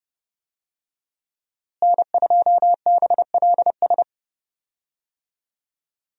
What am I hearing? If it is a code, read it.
Morse code, heard as a keyed tone, about 30 wpm, 700 Hz. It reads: N2BLS
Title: USA Call Signs - 5 Characters - Repeated 3x - 30wpm